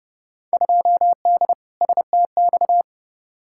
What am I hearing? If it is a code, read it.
Morse code, heard as a keyed tone, about 30 wpm, 700 Hz. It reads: 2D STX
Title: Field Day Exchanges - 30wpm